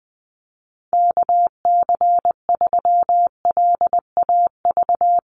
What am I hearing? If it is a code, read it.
Morse code, heard as a keyed tone, about 20 wpm, 700 Hz. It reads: KC3LAV
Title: USA Call Signs Repeated 3x 20wpm